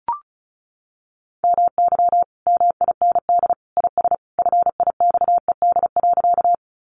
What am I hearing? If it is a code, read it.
Morse code, heard as a keyed tone, about 34 wpm, 700 Hz. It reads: MY MIND IS FIXED.
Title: Sentences from Top 1100 Words 35wpm